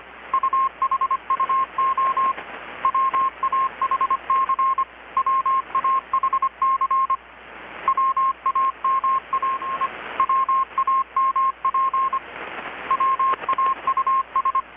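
Morse code, heard as a keyed tone, about 25 wpm, 1070 Hz. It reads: UHUO WAHC WAHC WAMAI WAMP WAUS
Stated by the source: Morse from KPH's Night of Nights XX maritime radiotelegraphy event, received in Chiapas, Mexico about 3,800 km away on a Kenwood R-600 with a grounded 10 m long-wire antenna.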